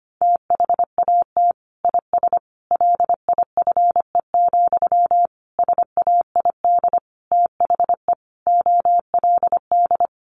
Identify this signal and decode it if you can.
Morse code, heard as a keyed tone, about 25 wpm, 700 Hz. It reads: THAT IS LIFE, SAID THE OLD
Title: Sentences from Top 900 Words 25wpm